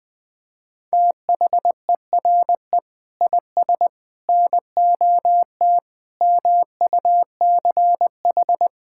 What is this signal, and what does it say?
Morse code, heard as a keyed tone, about 20 wpm, 700 Hz. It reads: THERE IS NOT MUCH
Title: Sentences from Top 1600 Words 20wpm